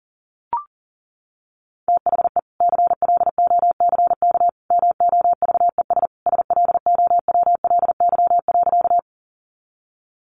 Morse code, heard as a keyed tone, about 37 wpm, 700 Hz. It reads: THE CLOCK MOVES SLOWLY.
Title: Easy Phrases 40wpm